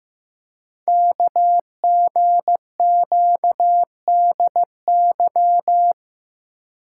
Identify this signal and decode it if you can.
Morse code, heard as a keyed tone, about 15 wpm, 700 Hz. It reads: KGQDY
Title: Code Groups - 5 Letters - 15wpm